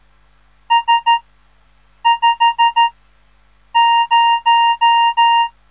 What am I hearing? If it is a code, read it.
Morse code, heard as a keyed tone, about 13 wpm, 930 Hz. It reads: S 5 0
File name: radiomorningstar / morse / les 3